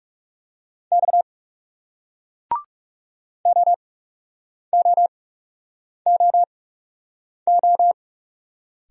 Morse code, heard as a keyed tone, about 39 wpm, spaced as for 8 wpm, 700 Hz. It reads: X OOOO